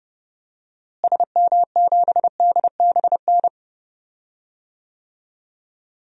Morse code, heard as a keyed tone, about 30 wpm, 700 Hz. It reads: SM7DBN
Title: DXCC-100 Call Signs - 6 Characters - Repeated 3x - 30wpm